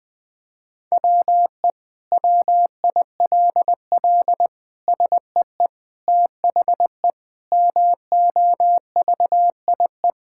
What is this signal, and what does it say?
Morse code, heard as a keyed tone, about 20 wpm, 700 Hz. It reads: WE WILL SEE THE MOVIE
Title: Sentences from Top 1000 Words - Review Part 5 - 20wpm